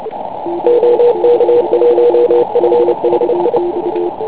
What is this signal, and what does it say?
Morse code, heard as a keyed tone, about 29 wpm, 480 Hz, with radio noise. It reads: OK1FHI
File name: Pile UP CUC204